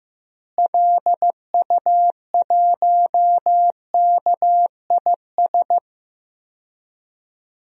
Morse code, heard as a keyed tone, about 15 wpm, 700 Hz. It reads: LU1KIS
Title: DXCC-100 Call Signs - 6 Characters - Repeated 2x - 15wpm